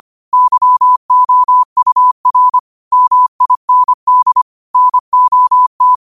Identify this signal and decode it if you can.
Morse code, heard as a keyed tone, about 25 wpm, 1000 Hz. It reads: YOUR MIND NOT
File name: qotdfiles / 25wpm